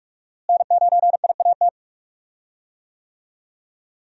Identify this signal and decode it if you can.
Morse code, heard as a keyed tone, about 46 wpm, 700 Hz. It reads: N9IAT
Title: USA Call Signs - 5 Characters - Repeated 2x - 45wpm